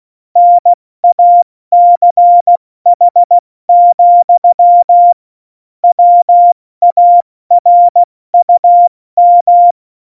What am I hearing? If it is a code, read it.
Morse code, heard as a keyed tone, about 16 wpm, 700 Hz. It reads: NACH, WARUM
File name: Morsekurs / Witze / Witz 08. Oktober 2019, 70 F80